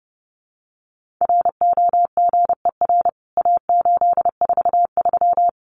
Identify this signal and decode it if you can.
Morse code, heard as a keyed tone, about 30 wpm, 700 Hz. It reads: ROGER A843
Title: CWT Exchange 30wpm